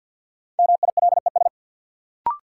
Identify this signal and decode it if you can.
Morse code, heard as a keyed tone, about 53 wpm, 700 Hz. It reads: MILES
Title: Sentences from Top 1000 Words - Review Part 3 - New Word Drill - 50wpm